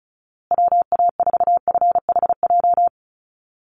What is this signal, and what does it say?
Morse code, heard as a keyed tone, about 34 wpm, 700 Hz. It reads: WA4FHJ
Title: USA Call Signs Repeated 2x 35wpm